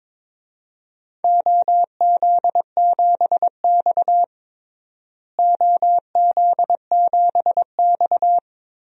Morse code, heard as a keyed tone, about 22 wpm, 700 Hz. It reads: OZ7X OZ7X
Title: DXCC-100 Call Signs - 4 Characters - Repeated 3x - 22wpm